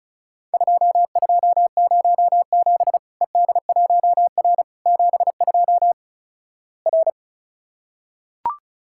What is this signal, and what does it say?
Morse code, heard as a keyed tone, about 36 wpm, 700 Hz. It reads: 2207 ED1R 72 R
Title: Worked All Europe DX Contest - QTC Exchange - 35wpm